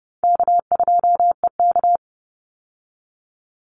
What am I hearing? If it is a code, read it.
Morse code, heard as a keyed tone, about 30 wpm, 700 Hz. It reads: K2EK
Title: Easy Call Signs - 30wpm